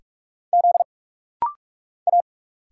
Morse code, heard as a keyed tone, about 46 wpm, 700 Hz. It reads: Z A
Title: Single Letter-Number Alt - Rapid-Fire 45wpm